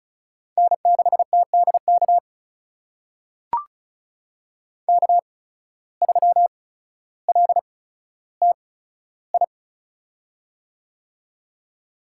Morse code, heard as a keyed tone, about 34 wpm, spaced as for 15 wpm, 700 Hz. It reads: N6TDK K3LTI
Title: USA Call Signs - 5 Characters - 8x Character Spacing - 35wpm